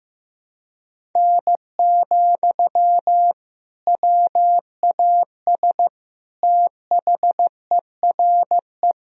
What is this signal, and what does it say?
Morse code, heard as a keyed tone, about 15 wpm, 700 Hz. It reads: N, WAS THERE
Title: Sentences from Top 100 Words and Abbreviations 15wpm